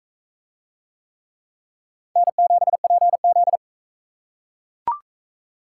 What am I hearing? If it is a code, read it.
Morse code, heard as a keyed tone, about 39 wpm, 700 Hz. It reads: N7PZ